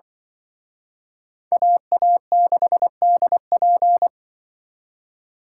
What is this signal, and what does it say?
Morse code, heard as a keyed tone, about 25 wpm, 700 Hz. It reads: AA6DP
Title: Speed Racing - USA 2x2 Call Signs 20wpm